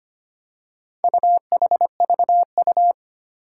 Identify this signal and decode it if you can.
Morse code, heard as a keyed tone, about 25 wpm, 700 Hz. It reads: UHVU
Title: Four Challenging Letter-Number Combinations 25wpm